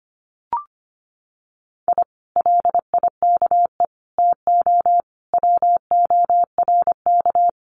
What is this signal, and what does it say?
Morse code, heard as a keyed tone, about 25 wpm, 700 Hz. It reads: I LIKE TO WORK
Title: Common QSO Phrases Encore 25wpm